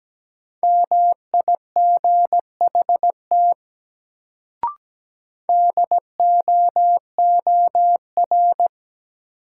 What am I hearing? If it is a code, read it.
Morse code, heard as a keyed tone, about 17 wpm, 700 Hz. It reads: MIGHT DOOR
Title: Sentences from Top 300 Words - New Word Drill - 17wpm